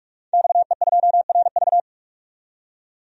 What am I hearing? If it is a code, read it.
Morse code, heard as a keyed tone, about 46 wpm, 700 Hz. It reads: XE2RV
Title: DXCC-100 Call Signs - 5 Characters - 45wpm